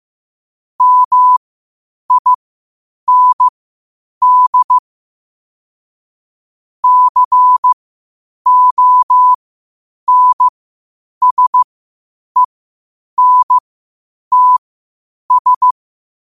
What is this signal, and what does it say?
Morse code, heard as a keyed tone, about 15 wpm, spaced as for 8 wpm, 1000 Hz. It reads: MIND CONSENTS
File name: qotdfiles / 8wpm